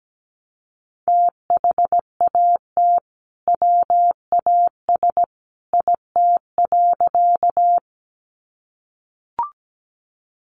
Morse code, heard as a keyed tone, about 17 wpm, 700 Hz. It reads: THAT WAS IT.
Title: Sentences from Top 100 Words 17wpm